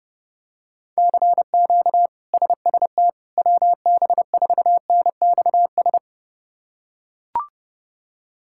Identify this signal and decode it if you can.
Morse code, heard as a keyed tone, about 28 wpm, 700 Hz. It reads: CQ SST WB4NXS
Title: K1USN SST QSO Practice - 30wpm